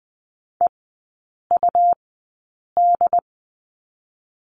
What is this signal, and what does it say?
Morse code, heard as a keyed tone, about 20 wpm, spaced as for 8 wpm, 690 Hz. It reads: EUD
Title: 205 Lesson 33e - Letter j - Three Characters 20wpm